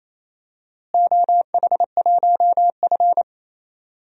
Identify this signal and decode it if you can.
Morse code, heard as a keyed tone, about 28 wpm, 700 Hz. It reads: OH1F
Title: DXCC-100 Call Signs - 4 Characters - Repeated 3x - 28wpm